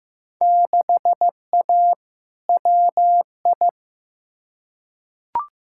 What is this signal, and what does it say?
Morse code, heard as a keyed tone, about 15 wpm, 700 Hz. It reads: 6A WI
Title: Field Day Exchanges - 15wpm